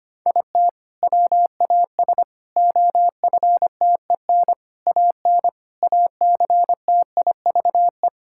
Morse code, heard as a keyed tone, about 25 wpm, 700 Hz. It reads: IT WAS OFTEN AN ACTIVE
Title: Sentences from Top 1000 Words - Review Part 6 - 25wpm